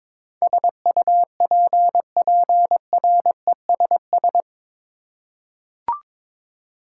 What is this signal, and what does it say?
Morse code, heard as a keyed tone, about 22 wpm, 700 Hz. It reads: SUPPRESS
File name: practice / 22wpm / Eight Letter Words - 22wpm